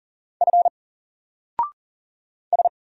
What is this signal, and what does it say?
Morse code, heard as a keyed tone, about 40 wpm, 700 Hz. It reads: F S
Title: Single Letter-Number Alt 40wpm